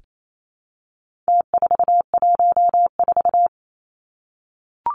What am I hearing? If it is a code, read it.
Morse code, heard as a keyed tone, about 28 wpm, 700 Hz. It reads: T414
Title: Numbers - Contest Style - 4-Digit 28wpm